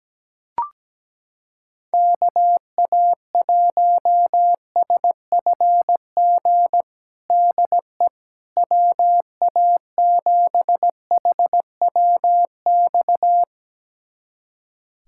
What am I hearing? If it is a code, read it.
Morse code, heard as a keyed tone, about 17 wpm, 700 Hz. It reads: KA1SFG DE WA7HWX
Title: USA Callsign DE Callsign 17wpm